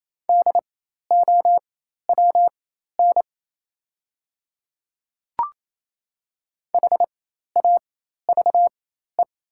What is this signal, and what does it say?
Morse code, heard as a keyed tone, about 28 wpm, spaced as for 13 wpm, 700 Hz. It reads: DOWN HAVE
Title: 4x Character Spacing - Top 100 Words - 28wpm